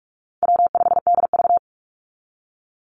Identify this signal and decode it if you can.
Morse code, heard as a keyed tone, about 44 wpm, 700 Hz. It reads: W5DV
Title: Call Signs Repeated 2x - 45wpm